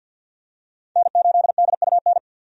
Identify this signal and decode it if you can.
Morse code, heard as a keyed tone, about 52 wpm, 700 Hz. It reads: N8DUN
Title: USA Call Signs - 5 Characters - 50wpm